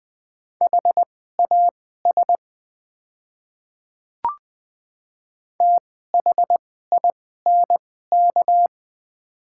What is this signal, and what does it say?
Morse code, heard as a keyed tone, about 20 wpm, spaced as for 15 wpm, 700 Hz. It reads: HAS THINK
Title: 2x Character Spacing - Top 200 Words - 20wpm